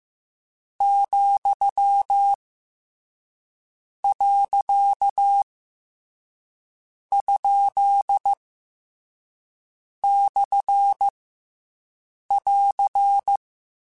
Morse code, heard as a keyed tone, about 15 wpm, 800 Hz. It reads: , . ? / <AR>
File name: Short Runs / Short Runs-mp3s / Words @13